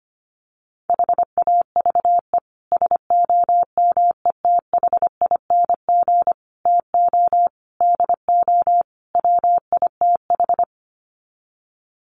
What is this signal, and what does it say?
Morse code, heard as a keyed tone, about 25 wpm, 700 Hz. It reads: HAVE SOMETHING TO DO WITH
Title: Sets of 5 Words Encore 25wpm